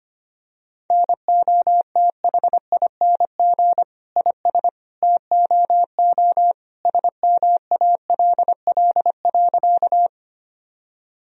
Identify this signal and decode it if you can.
Morse code, heard as a keyed tone, about 25 wpm, 700 Hz. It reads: NOTHING IS TOO SMALL.
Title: Sentences from Top 1000 Words - Review Part 1 - 25wpm